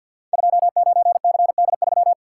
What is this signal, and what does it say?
Morse code, heard as a keyed tone, about 46 wpm, 700 Hz. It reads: 29CD3
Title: Code Groups - 5 Letter:Numbers - 50wpm